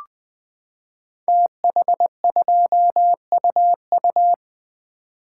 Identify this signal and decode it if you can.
Morse code, heard as a keyed tone, about 20 wpm, 700 Hz. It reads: TH2UU
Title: Code Groups - 5 Letter:Numbers - 20wpm